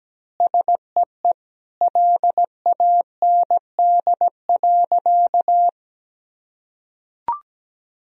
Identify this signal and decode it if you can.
Morse code, heard as a keyed tone, about 17 wpm, 700 Hz. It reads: SEE LAND.
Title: Sentences from Top 1000 Words - Review Part 2 - 17wpm